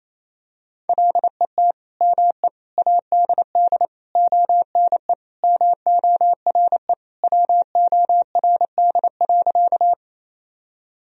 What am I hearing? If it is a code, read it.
Morse code, heard as a keyed tone, about 28 wpm, 700 Hz. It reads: LET ME ADD ONE MORE WORD.
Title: Sentences from Top 1000 Words - Review Part 1 - 28wpm